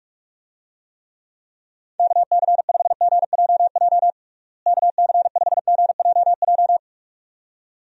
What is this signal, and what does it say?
Morse code, heard as a keyed tone, about 46 wpm, 700 Hz. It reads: KC5GJJ KC5GJJ
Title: USA Call Signs - 6 Characters - Repeated 2x - 45wpm